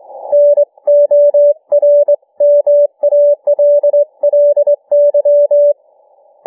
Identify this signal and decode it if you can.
Morse code, heard as a keyed tone, about 21 wpm, 590 Hz. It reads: NORMALLY